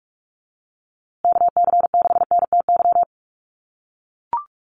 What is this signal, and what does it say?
Morse code, heard as a keyed tone, about 44 wpm, 700 Hz. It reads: KC6NTY
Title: USA Call Signs Repeated 2x 45wpm